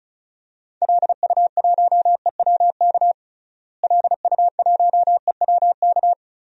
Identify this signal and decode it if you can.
Morse code, heard as a keyed tone, about 36 wpm, 700 Hz. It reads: LU1EWK LU1EWK
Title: DXCC-100 Call Signs - 6 Characters - Repeated 3x - 35wpm